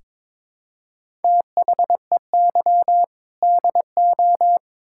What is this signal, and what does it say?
Morse code, heard as a keyed tone, about 22 wpm, 700 Hz. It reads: THEY DO
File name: Build A Phrase - 2-Word Phrases - 22wpm